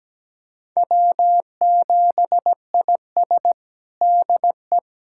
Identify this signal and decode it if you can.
Morse code, heard as a keyed tone, about 17 wpm, 700 Hz. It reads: W7IS DE
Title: USA Callsign DE Callsign - 4 Characters - 17wpm